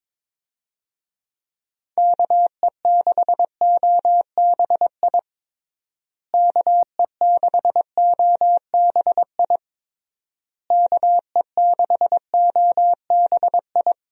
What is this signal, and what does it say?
Morse code, heard as a keyed tone, about 22 wpm, 700 Hz. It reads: KE6OBI KE6OBI KE6OBI
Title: USA Call Signs - 6 Characters - Repeated 3x - 22wpm